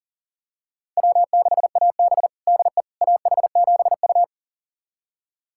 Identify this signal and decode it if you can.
Morse code, heard as a keyed tone, about 40 wpm, 700 Hz. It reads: W6AB DE AH7U
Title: USA Callsign DE Callsign - 4 Characters - 40wpm